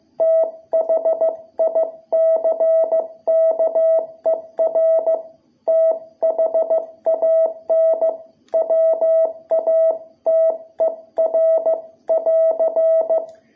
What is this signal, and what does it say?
Morse code, heard as a keyed tone, about 15 wpm, 660 Hz. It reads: THICKER THAN WATER <AR>
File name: jh8cbh / 2021cw